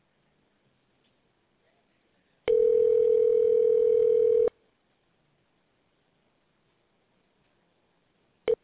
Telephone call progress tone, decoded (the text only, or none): ringback tone